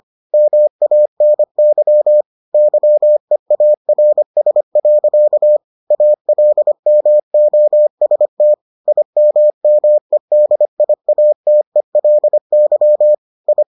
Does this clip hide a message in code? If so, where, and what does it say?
Yes: Morse code from 0.3 to 13.6 s, heard as a keyed tone, about 25 wpm, 600 Hz. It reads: MANY YEARS. ALMOST IMMEDIATELY I